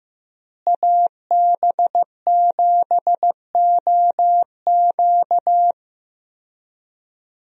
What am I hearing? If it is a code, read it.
Morse code, heard as a keyed tone, about 15 wpm, 700 Hz. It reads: AB7OQ
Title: USA Call Signs - 5 Characters - Repeated 3x - 15wpm